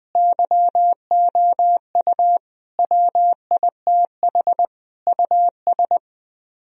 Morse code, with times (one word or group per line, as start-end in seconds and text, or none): 0.2-2.4 YOU
2.8-4.7 WITH
5.1-6.0 US